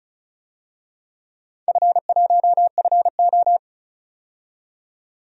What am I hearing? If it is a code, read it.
Morse code, heard as a keyed tone, about 34 wpm, 700 Hz. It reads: F1FO
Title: DXCC-100 Call Signs - 4 Characters - 35wpm